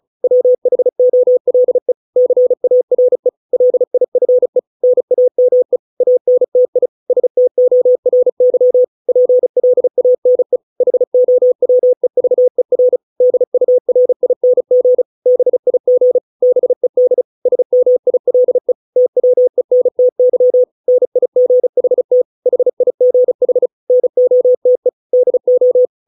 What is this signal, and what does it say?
Morse code, heard as a keyed tone, about 34 wpm, 500 Hz. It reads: WHOLE CARE LIFE NAME ANTI STORY PLANE HOWEVER DURING BIG BED SMILE TWENTY NIGHT HIGH NOTE DO